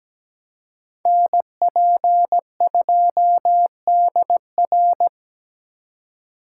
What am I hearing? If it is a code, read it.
Morse code, heard as a keyed tone, about 17 wpm, 700 Hz. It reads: NP2DR
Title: USA Call Signs - 5 Characters - 3x Character Spacing - 17wpm